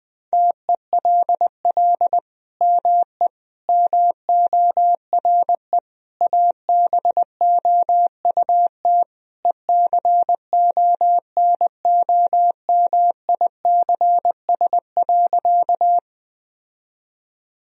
Morse code, heard as a keyed tone, about 20 wpm, 700 Hz. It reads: TELL ME MORE ABOUT ECONOMICS.
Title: Sentences from Top 1000 Words - Review Part 3 - 20wpm